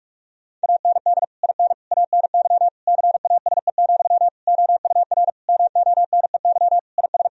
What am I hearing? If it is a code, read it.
Morse code, heard as a keyed tone, about 44 wpm, 700 Hz. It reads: AND IN ANY CASE, OUR MONEY IS
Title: Sentences from Top 1000 Words - Review Part 1 - 45wpm